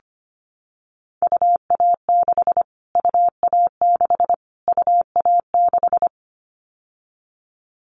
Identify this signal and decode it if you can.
Morse code, heard as a keyed tone, about 25 wpm, 700 Hz. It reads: UA6 UA6 UA6
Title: Callsign Prefixes for Top 25 DXCC - Repeated 3x - 25wpm